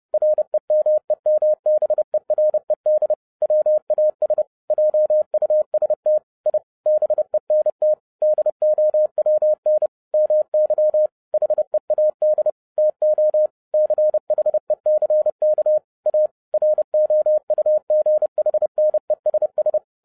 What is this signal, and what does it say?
Morse code, heard as a keyed tone, about 30 wpm, 610 Hz. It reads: REMEMBERED WAS JUST I BENT DOWN MY HEAD TO CHECK A ROUGHNESS
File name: 2011-07-mp3 / n55 BACK FROM A BIG ACCIDENT 30wpm